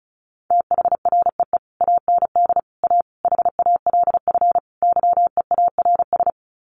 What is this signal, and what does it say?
Morse code, heard as a keyed tone, about 34 wpm, 700 Hz. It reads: THREE AND A HALF YEARS
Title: Sets of 5 Words Encore2 35wpm